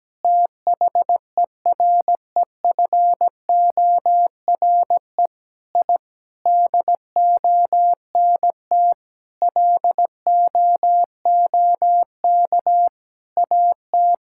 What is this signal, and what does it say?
Morse code, heard as a keyed tone, about 17 wpm, 700 Hz. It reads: THEREFORE I DONT LOOK AT
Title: Sentences from Top 1000 Words - Review Part 5 - 17wpm